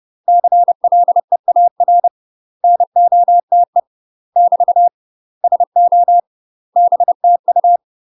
Morse code, heard as a keyed tone, about 30 wpm, 700 Hz. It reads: CLEAR NOTE <BT> SO BTU
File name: UK MASTER QSOs TRAINING / 30WPM / UK MASTER QSOs 0019